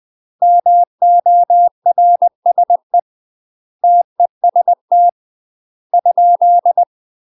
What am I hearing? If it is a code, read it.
Morse code, heard as a keyed tone, about 20 wpm, 700 Hz. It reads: MORSE TEST ?
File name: UK MASTER QSOs TRAINING / 20WPM / UK MASTER QSOs 0048